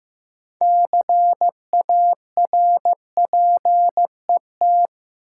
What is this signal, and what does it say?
Morse code, heard as a keyed tone, about 15 wpm, 700 Hz. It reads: CARPET
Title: Six Letter Words - 15wpm